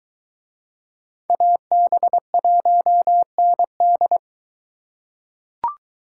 Morse code, heard as a keyed tone, about 22 wpm, 700 Hz. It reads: AB1ND